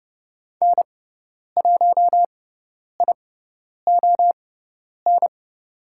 Morse code, heard as a keyed tone, about 30 wpm, spaced as for 10 wpm, 700 Hz. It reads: N1ION
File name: Lesson 16g - Letter c - Call Signs 30wpm